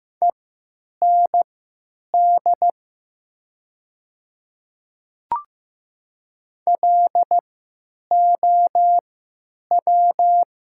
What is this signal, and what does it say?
Morse code, heard as a keyed tone, about 15 wpm, spaced as for 9 wpm, 700 Hz. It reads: END LOW
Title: 3x Character Spacing - Top 200 Words - 15wpm